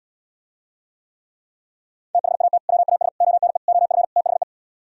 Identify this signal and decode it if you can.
Morse code, heard as a keyed tone, about 77 wpm, 700 Hz. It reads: THAT WAS ONE OF THE